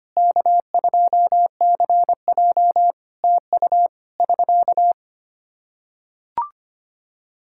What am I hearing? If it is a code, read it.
Morse code, heard as a keyed tone, about 25 wpm, 700 Hz. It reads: K2CJ TU <SK>
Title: K1USN SST QSO Practice 25wpm